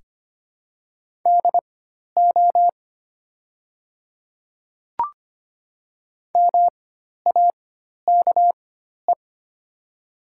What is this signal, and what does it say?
Morse code, heard as a keyed tone, about 25 wpm, spaced as for 12 wpm, 700 Hz. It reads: DO MAKE
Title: 4x Character Spacing - Top 100 Words - 25wpm